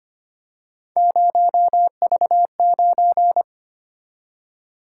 Morse code, heard as a keyed tone, about 25 wpm, 700 Hz. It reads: 0V9
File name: Three Letter-Number Combinations - 25wpm